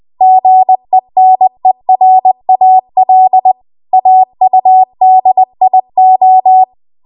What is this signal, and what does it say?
Morse code, heard as a keyed tone, about 20 wpm, 750 Hz. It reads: GENERAL AUDIO